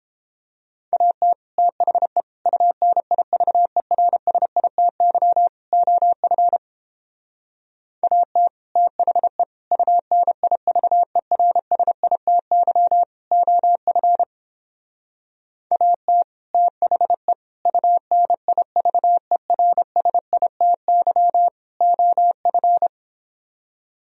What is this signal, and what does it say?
Morse code, heard as a keyed tone, about 31 wpm, 700 Hz. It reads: AT THE UNIVERSITY OF AT THE UNIVERSITY OF AT THE UNIVERSITY OF